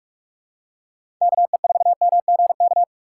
Morse code, heard as a keyed tone, about 43 wpm, 700 Hz. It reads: KE4MGK
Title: USA Call Signs - 4x Character Spacing - 45wpm